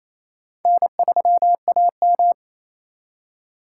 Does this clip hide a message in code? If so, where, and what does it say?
Yes: Morse code from 0.7 to 2.3 s, heard as a keyed tone, about 36 wpm, 700 Hz. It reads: N3AM